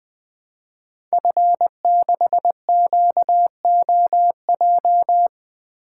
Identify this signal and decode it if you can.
Morse code, heard as a keyed tone, about 20 wpm, 700 Hz. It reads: F6QOJ
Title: Code Groups - 5 Letter:Numbers - 20wpm